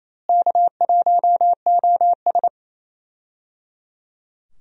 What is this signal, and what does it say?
Morse code, heard as a keyed tone, about 28 wpm, 700 Hz. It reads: K1OS
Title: USA Call Signs - 4 Characters - Repeated 3x - 28wpm